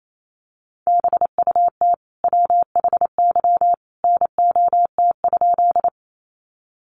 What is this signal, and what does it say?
Morse code, heard as a keyed tone, about 28 wpm, 700 Hz. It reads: BUT WHY NOT?
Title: Sentences from Top 600 Words 28wpm